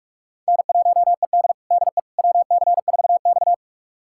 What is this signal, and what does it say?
Morse code, heard as a keyed tone, about 46 wpm, 700 Hz. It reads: N1ED DE WC4X
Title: USA Callsign DE Callsign - 4 Characters - 45wpm